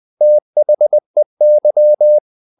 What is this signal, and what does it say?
Morse code, heard as a keyed tone, about 20 wpm, 600 Hz. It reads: THEY
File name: the old man and the sea part 2 20wpm